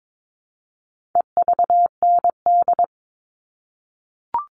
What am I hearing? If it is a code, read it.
Morse code, heard as a keyed tone, about 22 wpm, 700 Hz. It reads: EVND
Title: All Cut Numbers - 4-Digit 22wpm